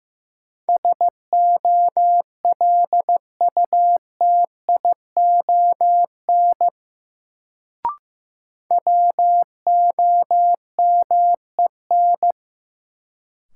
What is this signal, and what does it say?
Morse code, heard as a keyed tone, about 15 wpm, 700 Hz. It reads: SOLUTION WOMEN
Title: Sentences from Top 1300 Words - New Word Drill Alt - 15wpm